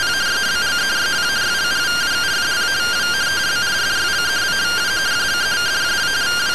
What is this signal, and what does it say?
Two-tone teleprinter signal, about 170 Hz shift.